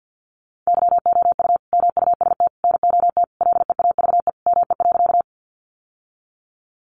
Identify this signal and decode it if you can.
Morse code, heard as a keyed tone, about 50 wpm, 700 Hz. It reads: YOU MUST NOT LEAVE ME.
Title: Sentences from Top 600 Words 50wpm